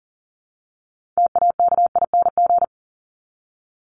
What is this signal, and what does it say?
Morse code, heard as a keyed tone, about 40 wpm, 700 Hz. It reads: TAKING
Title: ING Suffix - Top 100 Words with Base Words 40wpm